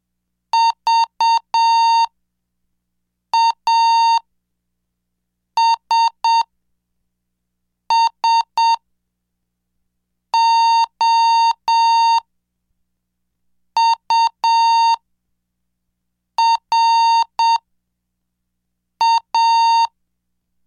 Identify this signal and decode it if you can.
Morse code, heard as a keyed tone, about 7 wpm, spaced as for 4 wpm, 920 Hz. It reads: VASSOURA
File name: cursos / CDestudo / 11 Track 11